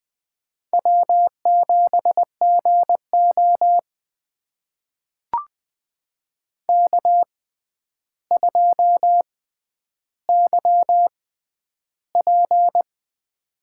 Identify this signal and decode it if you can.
Morse code, heard as a keyed tone, about 20 wpm, spaced as for 10 wpm, 700 Hz. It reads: W7GO K2YP
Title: USA Call Signs - 4 Characters - 6x Character Spacing - 20wpm